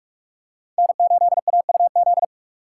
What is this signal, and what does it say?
Morse code, heard as a keyed tone, about 48 wpm, 700 Hz. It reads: N8AUZ
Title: USA Call Signs - 4x Character Spacing - 45wpm